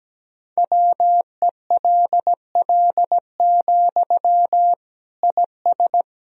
Morse code, heard as a keyed tone, about 17 wpm, 700 Hz. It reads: WELL, IS
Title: Sentences from Top 1000 Words - Review Part 1 - 17wpm